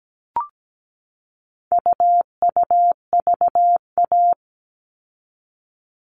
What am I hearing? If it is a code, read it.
Morse code, heard as a keyed tone, about 17 wpm, 700 Hz. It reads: UUVA
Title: All Cut Numbers - 4-Digit 17wpm